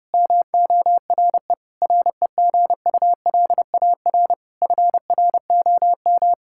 Morse code, heard as a keyed tone, about 30 wpm, 700 Hz. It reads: MORE REGULAR FROM
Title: Sentences from Top 1000 Words - Review Part 6 - 30wpm